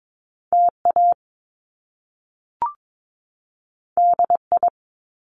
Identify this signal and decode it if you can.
Morse code, heard as a keyed tone, about 22 wpm, 700 Hz. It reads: TA DI
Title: Top Two Letter Combinations in Words 22wpm